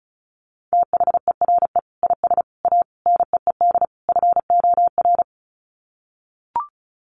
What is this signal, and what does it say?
Morse code, heard as a keyed tone, about 34 wpm, 700 Hz. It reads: THERE IS A NEED FOR
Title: Sets of 5 Words Encore2 35wpm